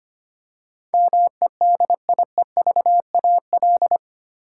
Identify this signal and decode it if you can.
Morse code, heard as a keyed tone, about 25 wpm, 700 Hz. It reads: MEDIEVAL
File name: Eight Letter Words - 25wpm